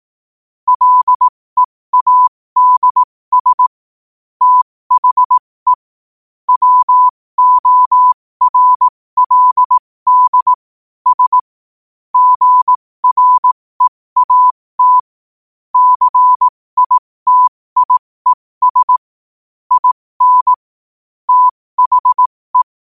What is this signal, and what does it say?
Morse code, heard as a keyed tone, about 18 wpm, spaced as for 15 wpm, 1000 Hz. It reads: LEADS THE WORLD S GREAT CITIES IN THE
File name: qotdfiles / 15wpm